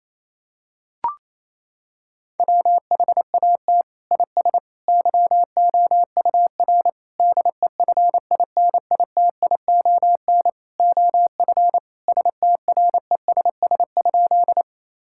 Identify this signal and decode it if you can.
Morse code, heard as a keyed tone, about 27 wpm, 700 Hz. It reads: WHAT IS YOUR DEFINITION OF STRESS?
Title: Sentences from Top 1000 Words - Review Part 6 - 28wpm